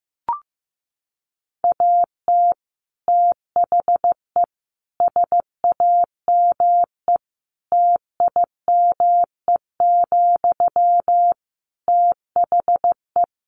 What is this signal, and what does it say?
Morse code, heard as a keyed tone, about 15 wpm, 700 Hz. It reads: AT THE SAME TIME, THE
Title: Sentences from Top 1500 Words 15wpm